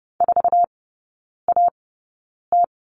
Morse code, heard as a keyed tone, about 30 wpm, spaced as for 9 wpm, 690 Hz. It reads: 4AT
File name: Lesson 16e - Letter c - Three Characters 30wpm